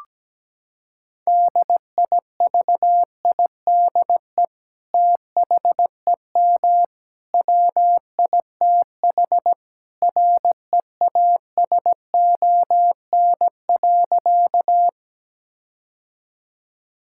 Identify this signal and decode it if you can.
Morse code, heard as a keyed tone, about 17 wpm, 700 Hz. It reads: DIVIDE THEM WITH REASON.
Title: Sentences from Top 1600 Words 17wpm